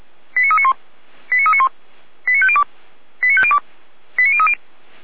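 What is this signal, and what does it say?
Five-tone selective call (ZVEI), heard as quick runs of five tones, tones of about 70 ms. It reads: 89391 89391 89502 89502 89030
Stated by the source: ZVEI five-tone selective call received with an SDR.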